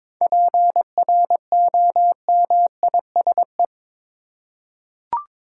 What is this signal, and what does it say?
Morse code, heard as a keyed tone, about 22 wpm, 700 Hz. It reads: PROMISE
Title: Seven Letter Words - 22wpm